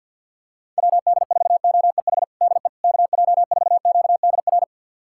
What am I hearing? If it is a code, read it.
Morse code, heard as a keyed tone, about 52 wpm, 700 Hz. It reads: WD4OEH DE KJ4QDR
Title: USA Callsign DE Callsign - 6 Characters - 50wpm